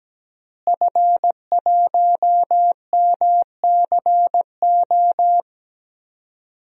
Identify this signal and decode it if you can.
Morse code, heard as a keyed tone, about 17 wpm, 700 Hz. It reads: F1MCO
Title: DXCC-100 Call Signs - 5 Characters - Repeated 2x - 17wpm